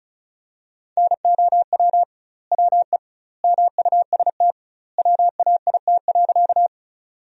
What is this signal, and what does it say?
Morse code, heard as a keyed tone, about 36 wpm, 700 Hz. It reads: NOW WE MUST WAIT.
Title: 2x Word-Spacing - Sentences from Top 500 Words - 35wpm